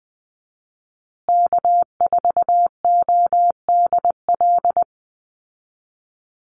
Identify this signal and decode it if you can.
Morse code, heard as a keyed tone, about 20 wpm, 700 Hz. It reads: K4ODL
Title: Call Signs Repeated 3x - 20wpm